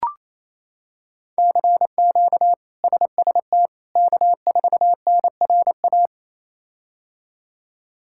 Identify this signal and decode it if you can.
Morse code, heard as a keyed tone, about 28 wpm, 700 Hz. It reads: CQ SST K4NRA
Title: K1USN SST QSO Practice - 28wpm